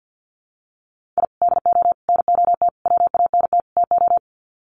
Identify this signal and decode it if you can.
Morse code, heard as a keyed tone, about 50 wpm, 700 Hz. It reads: I DO NOT WANT TO
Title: Sets of 5 Words 50wpm